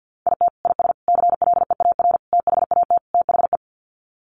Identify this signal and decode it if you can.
Morse code, heard as a keyed tone, about 50 wpm, 700 Hz. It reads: IT IS CLEAR THAT THE
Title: Sets of 5 Words Encore 50wpm